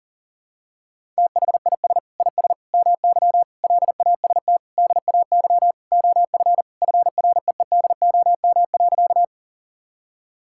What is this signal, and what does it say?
Morse code, heard as a keyed tone, about 40 wpm, 700 Hz. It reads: THIS IS MY LAST DAY OF FREEDOM.
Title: Sentences from Top 1000 Words - Review Part 5 - 40wpm